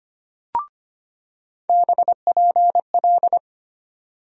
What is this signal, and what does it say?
Morse code, heard as a keyed tone, about 25 wpm, 700 Hz. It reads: BPL
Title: USA Call Sign Postfixes - 3 Letters 25wpm